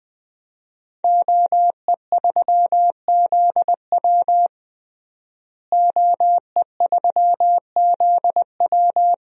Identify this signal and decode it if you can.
Morse code, heard as a keyed tone, about 19 wpm, 700 Hz. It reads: OE3ZW OE3ZW
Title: DXCC-100 Call Signs - 5 Characters - Repeated 3x - 20wpm